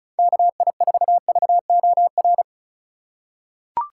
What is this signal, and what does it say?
Morse code, heard as a keyed tone, about 36 wpm, 700 Hz. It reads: KI4VOR
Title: USA Call Signs - 6 Characters - 4x Character Spacing - 35wpm